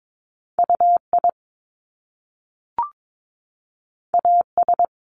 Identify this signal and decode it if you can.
Morse code, heard as a keyed tone, about 22 wpm, 700 Hz. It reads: UI AS
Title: Top Two Letter Combinations in Words 22wpm